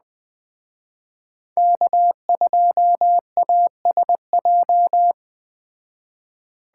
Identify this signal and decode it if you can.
Morse code, heard as a keyed tone, about 20 wpm, 700 Hz. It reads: K2ASJ
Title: Speed Racing - USA 1x3 Call Signs 17wpm